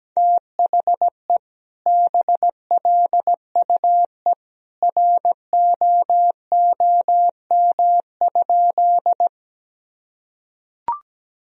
Morse code, heard as a keyed tone, about 17 wpm, 700 Hz. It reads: THE BLUE ROOM?
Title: Sentences from Top 1000 Words - Review Part 3 - 17wpm